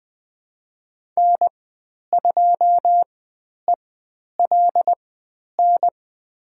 Morse code, heard as a keyed tone, about 20 wpm, spaced as for 10 wpm, 700 Hz. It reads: N2ELN
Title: 109 Lesson 18g - Letter m - Call Signs 20wpm